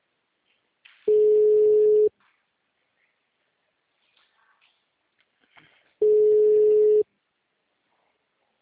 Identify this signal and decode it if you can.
Ringback tone (Europe, 425 Hz).
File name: out- -2019- - - .16080